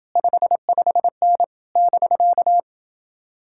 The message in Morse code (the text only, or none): 55N <BK>